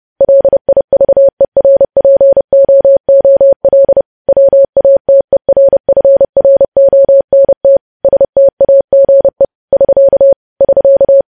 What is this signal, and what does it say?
Morse code, heard as a keyed tone, about 30 wpm, 550 Hz. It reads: LIVERPOOL WATERFRONT STAGE <SK> <SK>
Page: news headlines in morse code at 30 wpm